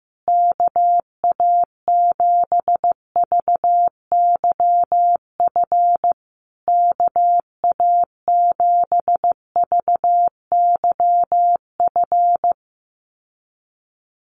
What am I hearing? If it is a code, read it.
Morse code, heard as a keyed tone, about 15 wpm, 700 Hz. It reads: KA7VYF KA7VYF
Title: USA Call Signs Repeated 3x 15wpm